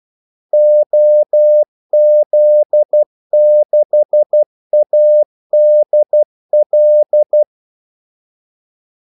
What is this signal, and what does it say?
Morse code, heard as a keyed tone, about 12 wpm, 600 Hz. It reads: OZ6ADL
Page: 2 ekstra ordmellemrum mellem kaldesignalerne